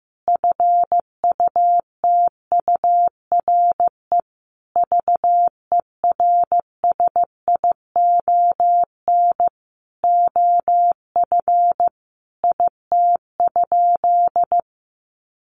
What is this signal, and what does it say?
Morse code, heard as a keyed tone, about 15 wpm, 700 Hz. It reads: FUTURE VERSION OF IT?
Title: Sentences from Top 1400 Words 15wpm